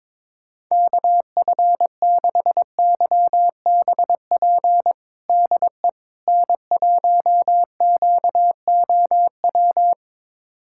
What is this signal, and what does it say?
Morse code, heard as a keyed tone, about 22 wpm, 700 Hz. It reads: KF6YBP DE N1QOW
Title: USA Callsign DE Callsign 22wpm